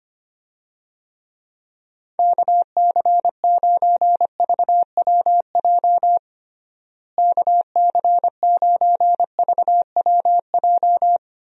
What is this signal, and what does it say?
Morse code, heard as a keyed tone, about 25 wpm, 700 Hz. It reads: KC9VWJ KC9VWJ
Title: USA Call Signs - 6 Characters - Repeated 3x - 25wpm